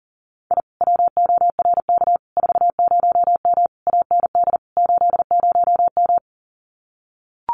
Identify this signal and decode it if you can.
Morse code, heard as a keyed tone, about 40 wpm, 700 Hz. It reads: I WORK 40M AND 80M
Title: Common QSO Phrases Encore 40wpm